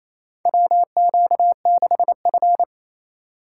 Morse code, heard as a keyed tone, about 28 wpm, 700 Hz. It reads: WQ6F
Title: USA Call Signs - 4 Characters - 2x Character Spacing - 28wpm